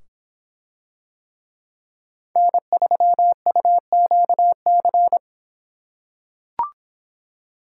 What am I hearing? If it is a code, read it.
Morse code, heard as a keyed tone, about 26 wpm, 700 Hz. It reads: N3UQC